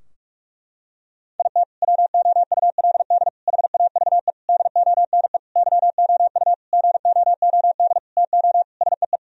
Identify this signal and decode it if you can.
Morse code, heard as a keyed tone, about 46 wpm, 700 Hz. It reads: IT WOULD HAVE DONE YOU GOOD TO SEE
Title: Sentences from Top 1000 Words - Review Part 2 - 45wpm